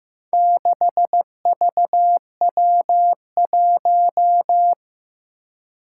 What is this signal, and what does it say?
Morse code, heard as a keyed tone, about 15 wpm, 700 Hz. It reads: 6VW1
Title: Four Challenging Letter-Number Combinations 15wpm